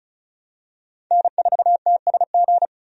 Morse code, heard as a keyed tone, about 36 wpm, 700 Hz. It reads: N4TSG
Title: USA Call Signs - 5 Characters - Repeated 3x - 35wpm